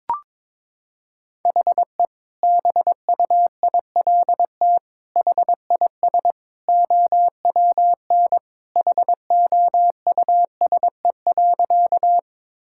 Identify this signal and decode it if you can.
Morse code, heard as a keyed tone, about 22 wpm, 700 Hz. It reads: HE BUILT HIS OWN HOUSE.
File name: practice / 22wpm / Sentences with 2-Word Phrases - 22wpm